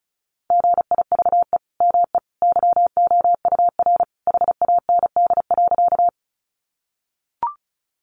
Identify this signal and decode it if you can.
Morse code, heard as a keyed tone, about 34 wpm, 700 Hz. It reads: GIVE ME YOUR HAND.
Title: Sentences from Top 200 Words 35wpm